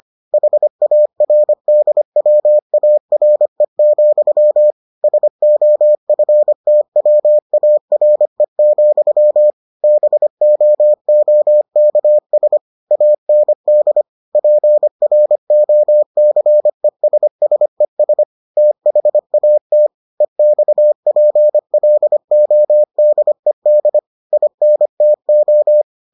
Morse code, heard as a keyed tone, about 25 wpm, 600 Hz. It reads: HARDWARE, SOFTWARE, BOOKS AND PROCESSES THAT EXPLODED INTO